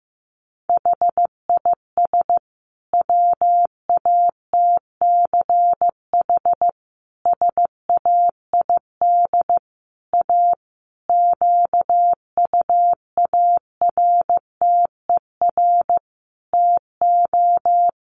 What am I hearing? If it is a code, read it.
Morse code, heard as a keyed tone, about 15 wpm, 700 Hz. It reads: HIS WATCH SAID A QUARTER TO
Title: Sentences from Top 1300 Words 15wpm